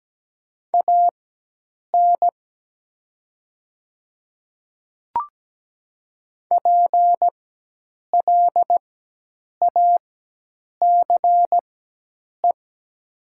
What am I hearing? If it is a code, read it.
Morse code, heard as a keyed tone, about 17 wpm, spaced as for 8 wpm, 700 Hz. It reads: AN PLACE
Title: 4x Character Spacing - Top 300 Words - 17wpm